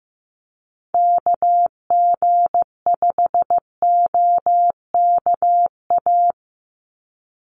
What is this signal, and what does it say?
Morse code, heard as a keyed tone, about 15 wpm, 700 Hz. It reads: KG5OKA
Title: USA Call Signs 15wpm